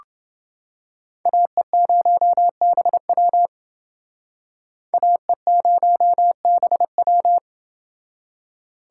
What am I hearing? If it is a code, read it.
Morse code, heard as a keyed tone, about 29 wpm, 700 Hz. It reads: AE0BW AE0BW